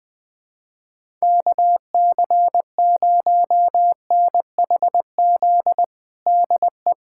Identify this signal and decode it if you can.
Morse code, heard as a keyed tone, about 20 wpm, 700 Hz. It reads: KC0NHZ DE
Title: USA Callsign DE Callsign - 6 Characters - 20wpm